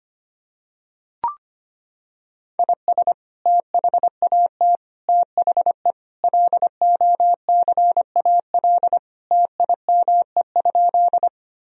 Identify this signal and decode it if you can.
Morse code, heard as a keyed tone, about 25 wpm, 700 Hz. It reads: IS THAT THE LOCAL TIME?
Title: Sentences from Top 1000 Words - Review Part 3 - 25wpm